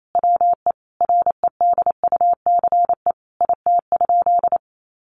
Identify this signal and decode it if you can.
Morse code, heard as a keyed tone, about 28 wpm, 700 Hz. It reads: WE REDUCE IT?
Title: Sentences from Top 1100 Words 28wpm